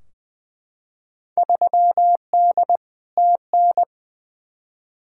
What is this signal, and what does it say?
Morse code, heard as a keyed tone, about 20 wpm, 700 Hz. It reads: 3D TN
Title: Field Day Exchanges Encore - 20wpm